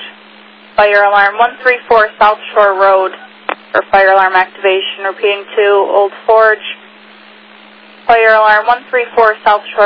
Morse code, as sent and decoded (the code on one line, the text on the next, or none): none